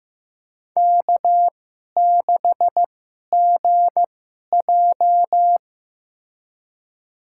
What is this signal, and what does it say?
Morse code, heard as a keyed tone, about 15 wpm, spaced as for 11 wpm, 700 Hz. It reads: K6GJ
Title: USA Call Signs - 2x Character Spacing - 15wpm